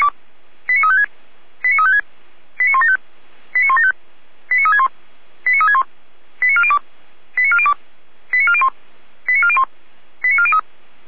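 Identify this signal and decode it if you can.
Five-tone selective call (ZVEI), heard as quick runs of five tones, tones of about 70 ms. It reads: 89356 89356 89175 89175 89361 89361 89402 89402 89401 89401 89403
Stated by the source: ZVEI five-tone selective call received on an SDR.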